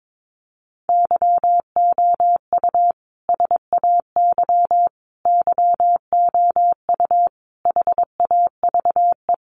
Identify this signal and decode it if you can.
Morse code, heard as a keyed tone, about 22 wpm, 700 Hz. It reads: YOU SAY YOU HAVE
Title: Sentences from Top 700 Words 22wpm